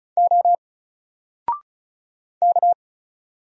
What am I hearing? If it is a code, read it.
Morse code, heard as a keyed tone, about 36 wpm, 700 Hz. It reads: O K
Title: Single Character (Letter:Number:Punctuation:Prosign) 35wpm